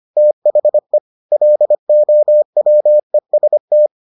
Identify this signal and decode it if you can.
Morse code, heard as a keyed tone, about 25 wpm, 600 Hz. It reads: THE LOWEST